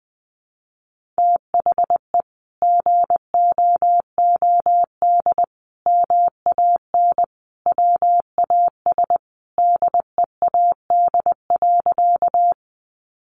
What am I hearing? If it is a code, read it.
Morse code, heard as a keyed tone, about 20 wpm, 700 Hz. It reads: THE GOOD MAN WAS DEAD.
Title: Sentences from Top 1500 Words 20wpm